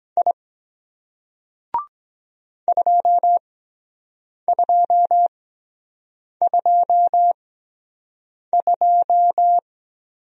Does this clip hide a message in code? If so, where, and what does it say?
Yes: Morse code from 0.2 to 9.6 s, heard as a keyed tone, about 23 wpm, spaced as for 7 wpm, 700 Hz. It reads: I 2222